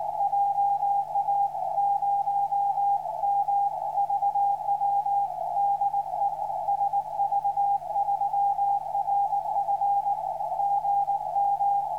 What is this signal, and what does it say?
Morse code, heard as a keyed tone, about 19 wpm, 800 Hz, with radio noise. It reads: JA0MVW DE SV3RF GM
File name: sv3rf 03-10-13 2029 160m